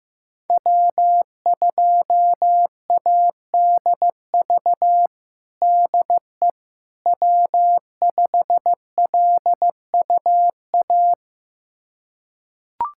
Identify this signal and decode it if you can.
Morse code, heard as a keyed tone, about 16 wpm, 700 Hz. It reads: W2ADV DE W5LUA
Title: USA Callsign DE Callsign - 5 Characters - 15wpm